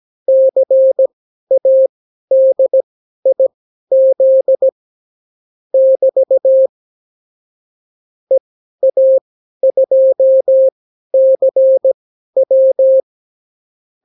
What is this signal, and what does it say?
Morse code, heard as a keyed tone, about 17 wpm, spaced as for 12 wpm, 530 Hz. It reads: CADIZ <BT> EA2CW
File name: CW QSO libroCW4 18-8ppm